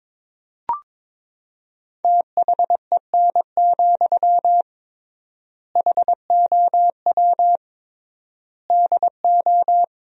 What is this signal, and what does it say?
Morse code, heard as a keyed tone, about 22 wpm, 700 Hz. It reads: THEN, HOW DO
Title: 3x Word-Spacing - Sentences from Top 100 Words - 22wpm